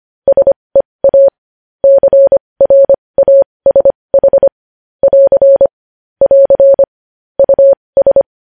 Morse code, heard as a keyed tone, about 25 wpm, spaced as for 20 wpm, 550 Hz. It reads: SEA CRASH <AR> <AR> US
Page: News Headlines in Morse Code at 20 WPM